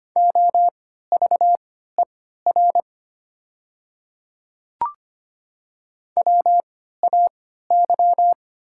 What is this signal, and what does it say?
Morse code, heard as a keyed tone, about 25 wpm, spaced as for 14 wpm, 700 Hz. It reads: OVER WAY
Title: 3x Character Spacing - Top 100 Words - 25wpm